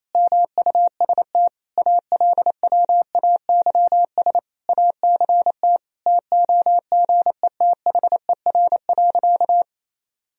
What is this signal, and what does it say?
Morse code, heard as a keyed tone, about 27 wpm, 700 Hz. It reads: MUST ALWAYS ACT TOGETHER.
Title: Sentences from Top 1000 Words - Review Part 1 - 28wpm